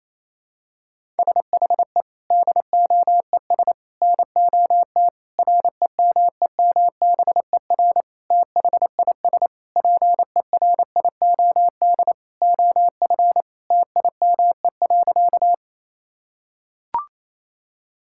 Morse code, heard as a keyed tone, about 28 wpm, 700 Hz. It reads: SHE DOES NOT REMEMBER THIS PERIOD OF TIME.
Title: Sentences from Top 1000 Words - Review Part 5 - 28wpm